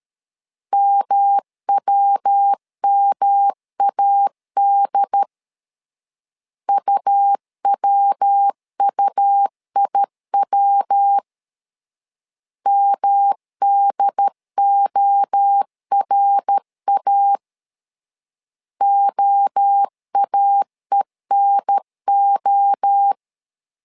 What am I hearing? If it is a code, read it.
Morse code, heard as a keyed tone, about 13 wpm, 800 Hz. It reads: MWMAD UWUIW MDORA OAENO
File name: LEZIONI / 3 MWRO / 3 MWRO 12wpm 1730